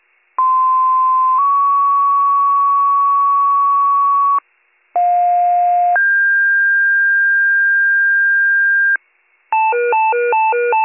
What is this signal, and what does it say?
Two-tone page, heard as a one-second tone, then a three-second tone, about 1040 then 1120 Hz, about 710 then 1600 Hz. It is heard from 0.4 to 9.0 s.